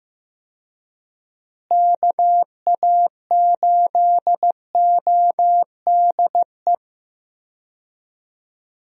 Morse code, heard as a keyed tone, about 15 wpm, 700 Hz. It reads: KA8ODE
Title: USA Call Signs - 6 Characters - Repeated 2x - 15wpm